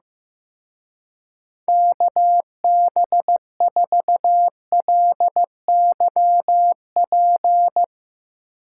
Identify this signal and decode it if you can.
Morse code, heard as a keyed tone, about 16 wpm, 700 Hz. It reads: KB4LYP
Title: Speed Racing - USA 2x3 Call Signs 15wpm